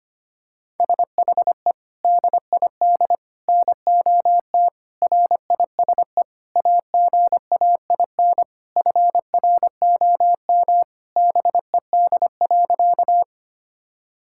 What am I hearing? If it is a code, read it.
Morse code, heard as a keyed tone, about 25 wpm, 700 Hz. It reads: SHE DID NOT RISE AGAIN FROM BED.
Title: Sentences from Top 1000 Words - Review Part 6 - 25wpm